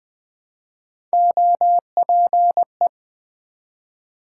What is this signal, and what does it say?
Morse code, heard as a keyed tone, about 20 wpm, 700 Hz. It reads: OPE
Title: Build A Word - 1 Letter At A Time - 3 to 5-Letter Words - 20wpm